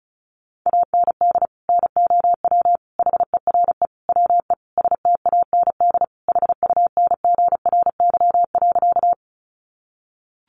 Morse code, heard as a keyed tone, about 34 wpm, 700 Hz. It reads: AND NOW HERE WE STAND HUNGRY.
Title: Sentences from Top 1400 Words 35wpm